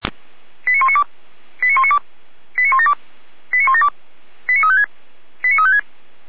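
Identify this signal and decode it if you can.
Five-tone selective call (ZVEI), heard as quick runs of five tones, tones of about 70 ms. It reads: 89192 89192 89172 89172 89356 89356
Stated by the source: ZVEI five-tone selective call 89127, received on an SDR, from a Feuerwehr Orlamünde fire brigade server.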